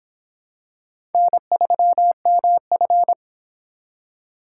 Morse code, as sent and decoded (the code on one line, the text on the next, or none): -. ...-- -- ..-.
N3MF